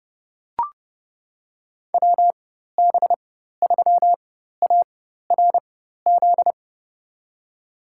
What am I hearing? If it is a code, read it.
Morse code, heard as a keyed tone, about 29 wpm, spaced as for 14 wpm, 700 Hz. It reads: WB3ARZ
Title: USA Call Signs - 6 Characters - 4x Character Spacing - 30wpm